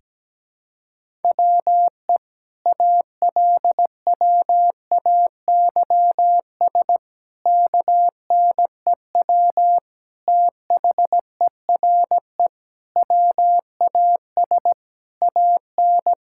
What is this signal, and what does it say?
Morse code, heard as a keyed tone, about 17 wpm, 700 Hz. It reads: WE ALWAYS KNEW THERE WAS AN
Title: Sentences from Top 1000 Words - Review Part 6 - 17wpm